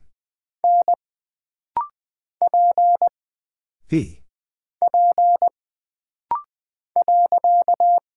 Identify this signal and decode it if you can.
Morse code, heard as a keyed tone, about 20 wpm, 700 Hz. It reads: N P P .
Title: Single Character (Letter:Number:Punctuation:Prosign) - Rapid-Fire - 20wpm